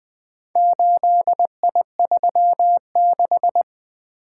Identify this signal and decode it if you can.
Morse code, heard as a keyed tone, about 20 wpm, 700 Hz. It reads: 8I36
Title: Four Letter-Number Combinations - 20wpm